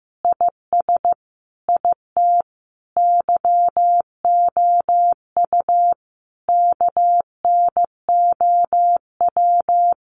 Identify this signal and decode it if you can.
Morse code, heard as a keyed tone, about 15 wpm, 700 Hz. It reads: IS IT YOU KNOW
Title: Sentences from Top 700 Words 15wpm